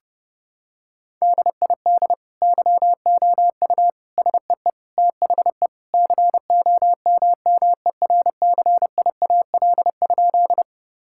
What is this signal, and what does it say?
Morse code, heard as a keyed tone, about 32 wpm, 700 Hz. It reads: DID YOU SEE THE COMMERCIAL?
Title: Sentences from Top 1000 Words - Review Part 5 - 30wpm